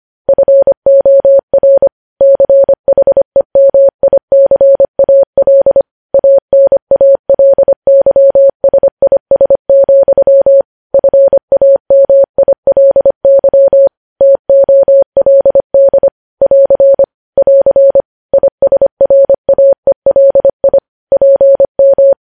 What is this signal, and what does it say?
Morse code, heard as a keyed tone, about 25 wpm, 550 Hz. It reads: FOR CHEMICAL ANALYSIS, FAMILY TOLD <AR> <AR> ISRAELI PM
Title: News Headlines at 25